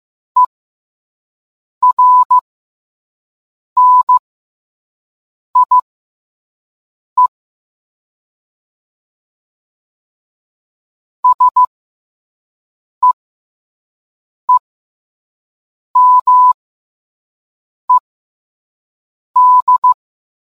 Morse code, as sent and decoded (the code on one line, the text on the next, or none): . .-. -. .. / ... . -- . -..
ERNIE SEEMED